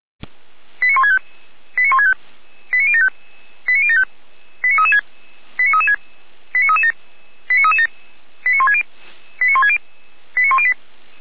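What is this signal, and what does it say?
Five-tone selective call (ZVEI), heard as quick runs of five tones, tones of about 80 ms. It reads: 89165 89165 89075 89075 89336 89337 89337 89160 89160 89108
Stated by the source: ZVEI five-tone selective call 89073 received by SDR, Feuerwehr Orlamünde.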